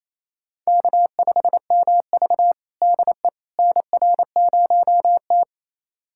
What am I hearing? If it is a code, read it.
Morse code, heard as a keyed tone, about 28 wpm, 700 Hz. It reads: K5MV DE NR0T
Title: USA Callsign DE Callsign - 4 Characters - 28wpm